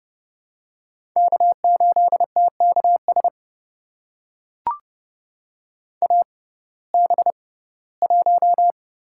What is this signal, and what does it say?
Morse code, heard as a keyed tone, about 30 wpm, spaced as for 18 wpm, 700 Hz. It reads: K8TKS AB1
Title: USA Call Signs - 5 Characters - 6x Character Spacing - 30wpm